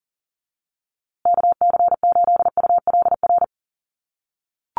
Morse code, heard as a keyed tone, about 40 wpm, 700 Hz. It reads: KC8ULR
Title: USA Call Signs Repeated 2x 40wpm